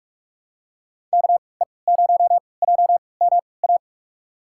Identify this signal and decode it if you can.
Morse code, heard as a keyed tone, about 41 wpm, spaced as for 25 wpm, 700 Hz. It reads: KE0JMA
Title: USA Call Signs - 3x Character Spacing - 45wpm